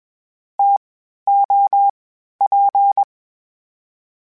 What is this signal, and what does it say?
Morse code, heard as a keyed tone, about 21 wpm, 800 Hz. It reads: T O P
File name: Variable Spacing / Variable Spacing-mp3s / TOs @20